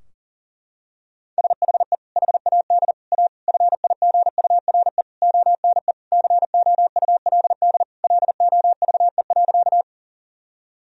Morse code, heard as a keyed tone, about 40 wpm, 700 Hz. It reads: SHE HAD A FIGURE ONE COULD LOVE.
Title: Sentences from Top 1000 Words - Review Part 2 - 40wpm